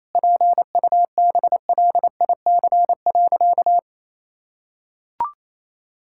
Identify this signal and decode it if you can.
Morse code, heard as a keyed tone, about 28 wpm, 700 Hz. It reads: PUBLIC.
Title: Sentences from Top 1000 Words - Review Part 6 - 28wpm